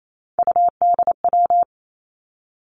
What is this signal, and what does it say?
Morse code, heard as a keyed tone, about 28 wpm, 700 Hz. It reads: UDW
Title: Three Challenging Letter-Number Combinations 28wpm